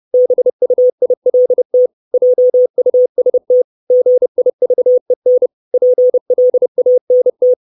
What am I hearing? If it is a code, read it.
Morse code, heard as a keyed tone, about 30 wpm, 500 Hz. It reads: BUILT JUST GIVEN PLANT
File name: 632 Words 30WPM